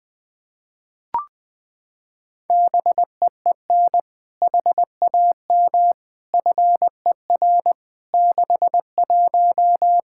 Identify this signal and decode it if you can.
Morse code, heard as a keyed tone, about 20 wpm, 700 Hz. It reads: BEEN HAM FER 61
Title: Practice QSO - Exchange 3 - More Info - 20wpm